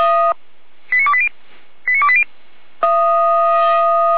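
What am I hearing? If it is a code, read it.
Five-tone selective call (ZVEI), heard as quick runs of five tones, tones of about 70 ms. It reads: 89280 89280
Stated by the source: ZVEI five-tone selective call received on SDR.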